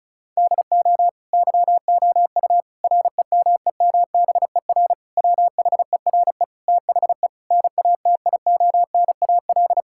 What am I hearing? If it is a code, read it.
Morse code, heard as a keyed tone, about 36 wpm, 700 Hz. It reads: DO YOU REMEMBER WHERE THE NATIONAL
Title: Sentences from Top 1000 Words - Review Part 6 - 35wpm